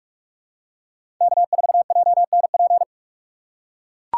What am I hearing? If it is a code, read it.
Morse code, heard as a keyed tone, about 43 wpm, 700 Hz. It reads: K4JNP
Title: USA Call Signs - 5 Characters - 6x Character Spacing - 45wpm